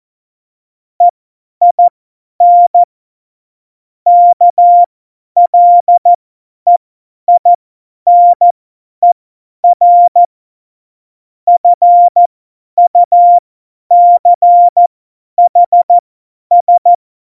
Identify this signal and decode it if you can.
Morse code, heard as a keyed tone, about 14 wpm, spaced as for 10 wpm, 700 Hz. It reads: EIN KLEINER FUCHS
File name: Morsekurs / Witze / Witz 03. November 2020, 50 F70